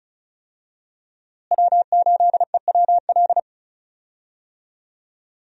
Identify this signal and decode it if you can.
Morse code, heard as a keyed tone, about 36 wpm, 700 Hz. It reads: W8EWL
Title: USA Call Signs - 5 Characters - Repeated 3x - 35wpm